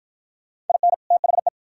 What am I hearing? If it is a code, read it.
Morse code, heard as a keyed tone, about 53 wpm, 700 Hz. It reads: IN THE